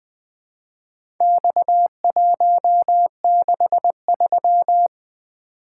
Morse code, heard as a keyed tone, about 21 wpm, 700 Hz. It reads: X163
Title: Speed-Racing - Four Letter-Number Combinations - 20wpm